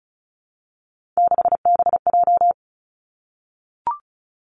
Morse code, heard as a keyed tone, about 36 wpm, 700 Hz. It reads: 6BJ
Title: Three Challenging Letter-Number Combinations 35wpm